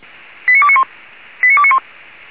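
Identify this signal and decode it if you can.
Five-tone selective call (ZVEI), heard as quick runs of five tones, tones of about 70 ms. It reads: 89291 89291
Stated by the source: Five-tone ZVEI selective call received by SDR.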